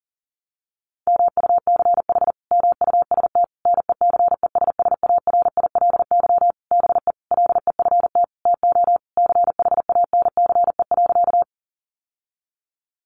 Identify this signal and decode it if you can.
Morse code, heard as a keyed tone, about 40 wpm, 700 Hz. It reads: MUCH MUST NECESSARILY BE LEFT TO CHANCE.
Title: Sentences from Top 1400 Words 40wpm